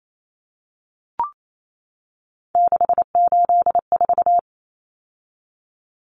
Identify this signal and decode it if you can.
Morse code, heard as a keyed tone, about 28 wpm, 700 Hz. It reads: 684
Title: Numbers - 3-Digit 28wpm